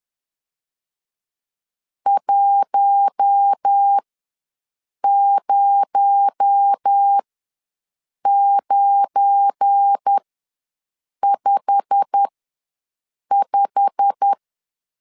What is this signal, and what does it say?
Morse code, heard as a keyed tone, about 10 wpm, spaced as for 6 wpm, 800 Hz. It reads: 10955
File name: LEZIONI / 8 YJ45 / 8 YJ45 ws 10wpm 3843